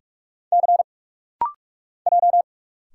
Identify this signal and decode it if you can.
Morse code, heard as a keyed tone, about 46 wpm, 700 Hz. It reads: C J
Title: Single Letters - Rapid-Fire 45wpm